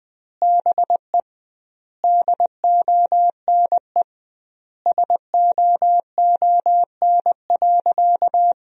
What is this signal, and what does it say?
Morse code, heard as a keyed tone, about 20 wpm, 700 Hz. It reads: BE DONE SOON.
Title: 2x Word-Spacing - Sentences from Top 500 Words - 20wpm